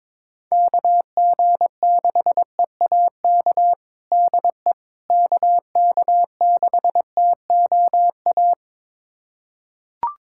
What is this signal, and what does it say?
Morse code, heard as a keyed tone, about 22 wpm, 700 Hz. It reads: KG6EAK DE KK6TOA
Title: USA Callsign DE Callsign - 6 Characters - 22wpm